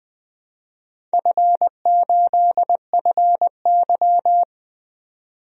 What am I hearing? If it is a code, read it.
Morse code, heard as a keyed tone, about 20 wpm, 700 Hz. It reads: F8FY
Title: DXCC-100 Call Signs - 4 Characters - 20wpm